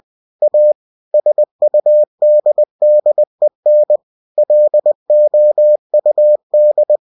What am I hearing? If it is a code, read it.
Morse code, heard as a keyed tone, about 20 wpm, 600 Hz. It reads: A SUDDEN LOUD
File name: cw-books / winnie-the-pooh / winnie-20wpm-page-0101